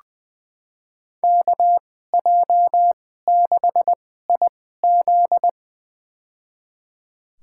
Morse code, heard as a keyed tone, about 20 wpm, spaced as for 14 wpm, 700 Hz. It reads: KJ6IZ
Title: USA Call Signs - 2x Character Spacing - 20wpm